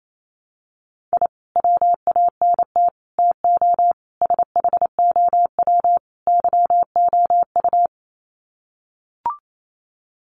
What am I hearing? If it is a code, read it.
Morse code, heard as a keyed tone, about 28 wpm, 700 Hz. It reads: I WANT TO SHOW YOU
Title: Sets of 5 Words 28wpm